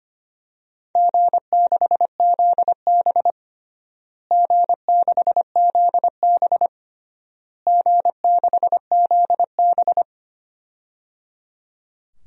Morse code, heard as a keyed tone, about 25 wpm, 700 Hz. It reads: G6ZB G6ZB G6ZB
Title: DXCC-100 Call Signs - 4 Characters - Repeated 3x - 25wpm